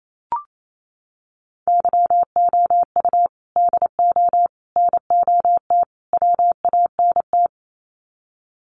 Morse code, heard as a keyed tone, about 28 wpm, 700 Hz. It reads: YOU DO NOT WANT
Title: Sets of 4 Words 28wpm